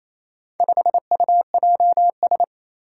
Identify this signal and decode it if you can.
Morse code, heard as a keyed tone, about 28 wpm, 700 Hz. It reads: 5UJS
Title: Four Challenging Letter-Number Combinations 28wpm